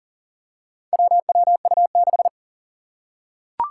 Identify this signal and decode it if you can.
Morse code, heard as a keyed tone, about 40 wpm, 700 Hz. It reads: WWU6
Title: Four Challenging Letter-Number Combinations 40wpm